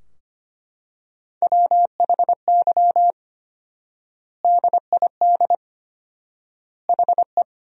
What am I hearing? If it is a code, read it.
Morse code, heard as a keyed tone, about 25 wpm, 700 Hz. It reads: WHY DID HE
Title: 4x Word-Spacing - Sentences from Top 200 Words - 25wpm